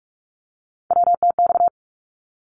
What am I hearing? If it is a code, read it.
Morse code, heard as a keyed tone, about 44 wpm, 700 Hz. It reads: WTX
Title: ARRL Sections 45wpm